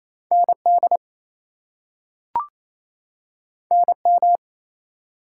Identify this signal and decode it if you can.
Morse code, heard as a keyed tone, about 28 wpm, 700 Hz. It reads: ND NM
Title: US States and Canadian Provinces 28wpm